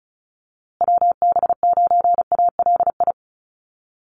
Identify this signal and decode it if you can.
Morse code, heard as a keyed tone, about 34 wpm, 700 Hz. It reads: WB9ALI
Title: USA Call Signs Repeated 3x 35wpm